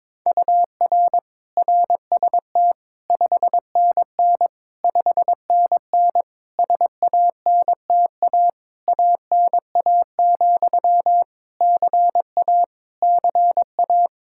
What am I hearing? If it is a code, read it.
Morse code, heard as a keyed tone, about 21 wpm, 700 Hz. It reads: UR RST 5NN 5NN SANTA ANA, CA CA
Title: Practice QSO - Exchange 1 - Beginning - 22wpm